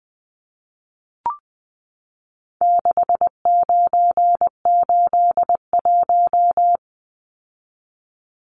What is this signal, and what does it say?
Morse code, heard as a keyed tone, about 20 wpm, 700 Hz. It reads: 6981
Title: Numbers - 4-Digit 20wpm